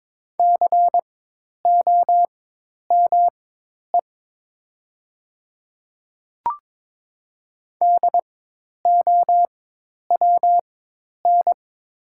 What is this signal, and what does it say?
Morse code, heard as a keyed tone, about 22 wpm, spaced as for 10 wpm, 700 Hz. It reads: COME DOWN
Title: 4x Character Spacing - Top 100 Words - 22wpm